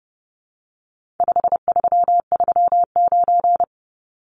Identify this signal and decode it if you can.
Morse code, heard as a keyed tone, about 30 wpm, 700 Hz. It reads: 5339
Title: Numbers - 4-Digit 30wpm